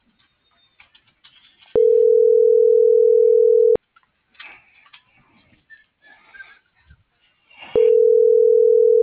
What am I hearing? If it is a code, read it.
Ringback tone (North America), 440 + 480 Hz.